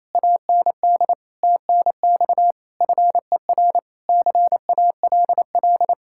Morse code, heard as a keyed tone, about 34 wpm, 700 Hz. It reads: AND TNX FER CALL
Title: Speed-Racing Common QSO Phrases Encore - 28wpm